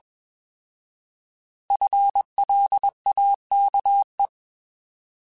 Morse code, heard as a keyed tone, about 22 wpm, 800 Hz. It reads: FLAKE